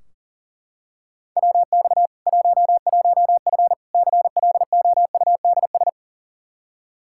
Morse code, heard as a keyed tone, about 40 wpm, 700 Hz. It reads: WX 11F CLOUDS
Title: Practice QSO - Exchange 2 - Info - 40wpm